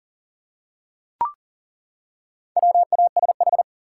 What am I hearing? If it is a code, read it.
Morse code, heard as a keyed tone, about 40 wpm, 700 Hz. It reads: WASH
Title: Four Letter Words - 40wpm